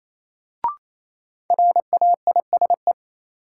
Morse code, heard as a keyed tone, about 28 wpm, 700 Hz. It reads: RAISE
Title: Sentences from Top 1400 Words - New Word Drill Alt - 28wpm